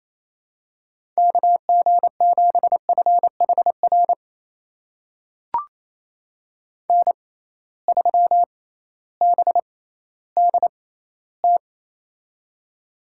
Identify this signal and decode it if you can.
Morse code, heard as a keyed tone, about 28 wpm, spaced as for 15 wpm, 700 Hz. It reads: KG7FHR N3BDT
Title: USA Call Signs - 6x Character Spacing - 28wpm